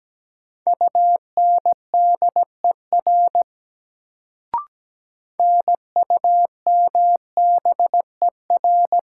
Morse code, heard as a keyed tone, about 17 wpm, 700 Hz. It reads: UNDER NUMBER
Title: Sets of 2 Words - New Word Drill - 17wpm